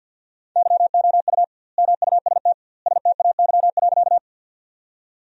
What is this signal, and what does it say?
Morse code, heard as a keyed tone, about 50 wpm, 700 Hz. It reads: YOU MUST STAY.
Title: 2x Word-Spacing - Sentences from Top 500 Words - 50wpm